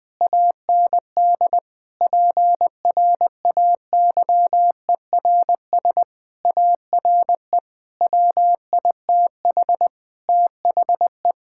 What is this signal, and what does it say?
Morse code, heard as a keyed tone, about 21 wpm, 700 Hz. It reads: AND PRAYERS ARE WITH THE
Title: Sentences with 7-Word Phrases - 20wpm